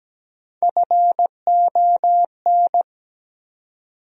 Morse code, heard as a keyed tone, about 17 wpm, 700 Hz. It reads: FON
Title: USA Call Sign Postfixes - 3 Letters 17wpm